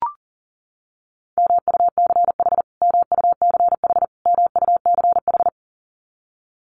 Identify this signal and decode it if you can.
Morse code, heard as a keyed tone, about 40 wpm, 700 Hz. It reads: MUCH MUCH MUCH
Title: Top 500 Words Repeated 40wpm